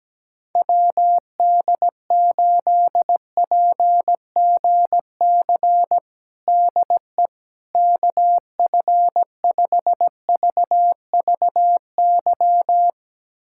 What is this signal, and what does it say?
Morse code, heard as a keyed tone, about 17 wpm, 700 Hz. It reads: WD8PGC DE KF5VVY
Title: USA Callsign DE Callsign - 6 Characters - 17wpm